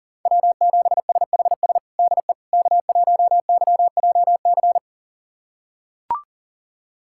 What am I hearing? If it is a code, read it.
Morse code, heard as a keyed tone, about 40 wpm, 700 Hz. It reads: W7SHS DE K1YJC
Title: USA Callsign DE Callsign - 5 Characters - 40wpm